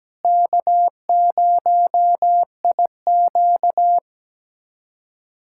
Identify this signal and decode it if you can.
Morse code, heard as a keyed tone, about 17 wpm, 700 Hz. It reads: K0IQ
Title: USA Callsign DE Callsign - 4 Characters - 17wpm